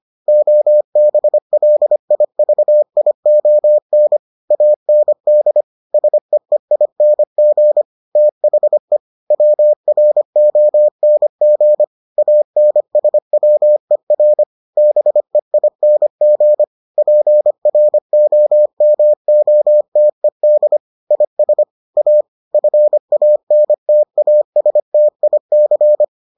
Morse code, heard as a keyed tone, about 25 wpm, 600 Hz. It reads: OBLIVION AND SEEING THE WRONG ANSWER BEING PROMOTED IS A FANTASTIC